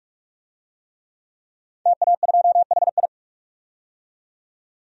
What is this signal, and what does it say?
Morse code, heard as a keyed tone, about 43 wpm, 700 Hz. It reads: TA2HI
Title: DXCC-100 Call Signs - 5 Characters - 45wpm